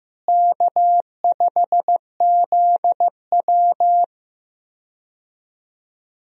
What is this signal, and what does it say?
Morse code, heard as a keyed tone, about 16 wpm, 700 Hz. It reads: K5ZW
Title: USA Call Signs - 4 Characters - Repeated 3x - 15wpm